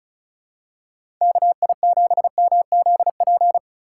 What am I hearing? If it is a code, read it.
Morse code, heard as a keyed tone, about 34 wpm, 700 Hz. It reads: KI7MZP
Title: USA Call Signs - 6 Characters - 3x Character Spacing - 35wpm